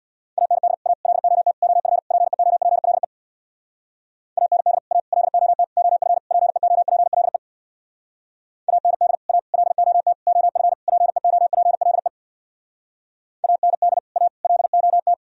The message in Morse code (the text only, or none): AND A LOT OF PEOPLE AND A LOT OF PEOPLE AND A LOT OF PEOPLE AND A LOT